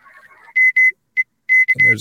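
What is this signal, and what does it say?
Morse code, heard as a keyed tone, about 26 wpm, spaced as for 18 wpm, 2050 Hz, with speech. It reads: MEK
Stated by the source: Live amateur radio HF contacts attempted on the air.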